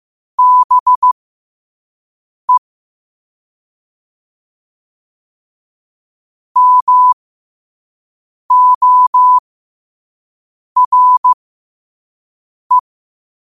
Morse code, heard as a keyed tone, about 15 wpm, spaced as for 5 wpm, 1000 Hz. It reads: BE MORE
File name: qotdfiles / 5wpm